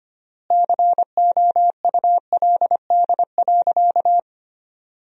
Morse code, heard as a keyed tone, about 25 wpm, 700 Hz. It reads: COULD.
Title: 3x Word-Spacing - Sentences from Top 500 Words - 25wpm